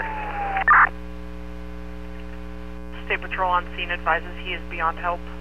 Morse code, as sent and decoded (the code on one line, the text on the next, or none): none